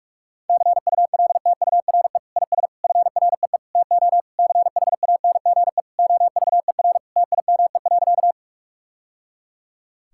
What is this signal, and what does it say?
Morse code, heard as a keyed tone, about 46 wpm, 700 Hz. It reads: CULTURE IS FREE TO CHANGE OVER TIME.
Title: Sentences from Top 1000 Words - Review Part 6 - 45wpm